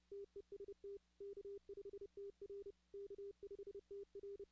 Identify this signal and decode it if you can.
Morse code, heard as a keyed tone, about 30 wpm, 400 Hz. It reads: TEST K5TR K5TR
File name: 2016 arrl dx cw / 2016-ARRL-DX-CW-K5TR- - 0-46